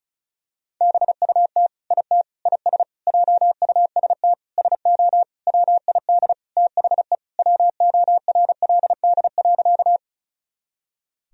Morse code, heard as a keyed tone, about 35 wpm, 700 Hz. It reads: BUT IT IS JUST SO WID THE WORLD.
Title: Sentences from Top 200 Words and Abbreviations 35wpm